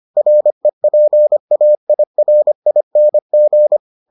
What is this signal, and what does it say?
Morse code, heard as a keyed tone, about 25 wpm, 600 Hz. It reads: REPAIRING